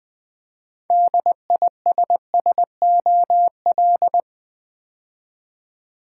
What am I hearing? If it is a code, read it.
Morse code, heard as a keyed tone, about 20 wpm, 700 Hz. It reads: DISSOL
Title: Build A Word - 2 Letters At A Time - 8-Letter Words - 20wpm